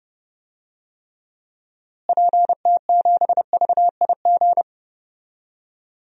Morse code, heard as a keyed tone, about 30 wpm, 700 Hz. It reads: PT7VIG
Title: DXCC-100 Call Signs - 6 Characters - Repeated 3x - 30wpm